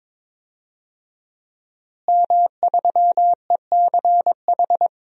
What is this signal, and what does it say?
Morse code, heard as a keyed tone, about 22 wpm, 700 Hz. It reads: M3ECH
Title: DXCC-100 Call Signs - 5 Characters - Repeated 2x - 22wpm